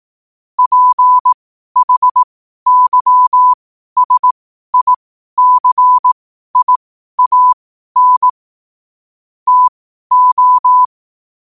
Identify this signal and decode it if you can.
Morse code, heard as a keyed tone, about 18 wpm, spaced as for 12 wpm, 1000 Hz. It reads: PHYSICIAN TO
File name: qotdfiles / 12wpm